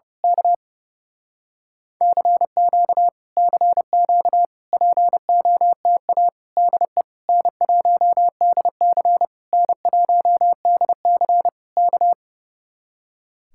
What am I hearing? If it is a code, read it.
Morse code, heard as a keyed tone, about 28 wpm, 700 Hz. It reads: K CQ CQ POTA DE N1DC N1DC K
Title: Speed-Racing Common QSO Phrases Encore - 30wpm